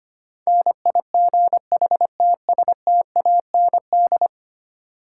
Morse code, heard as a keyed tone, about 26 wpm, 700 Hz. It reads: NIGHTSTAND
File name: Easy Words - Speed-Racing with Hints - 25wpm